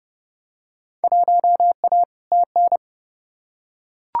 Morse code, heard as a keyed tone, about 30 wpm, 700 Hz. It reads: 1A TN
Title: Field Day Exchanges - 30wpm